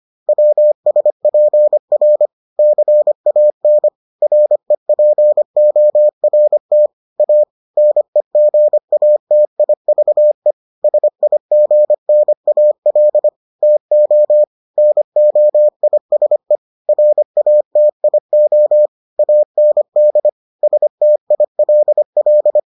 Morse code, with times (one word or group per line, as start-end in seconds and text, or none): 0.3-2.3 WSPR
2.6-3.9 CAN
4.2-6.9 REPORT
7.2-7.4 A
7.8-10.5 NEGATIVE
10.8-13.3 SIGNAL
13.6-14.4 TO
14.8-16.6 NOISE
16.9-18.9 RATIO
19.2-20.3 AND
20.6-22.6 STILL